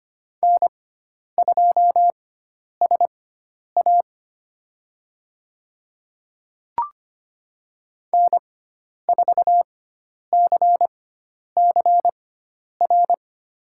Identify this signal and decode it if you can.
Morse code, heard as a keyed tone, about 25 wpm, spaced as for 10 wpm, 700 Hz. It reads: N2SA N4CCR
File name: Lesson 16g - Letter c - Call Signs 25wpm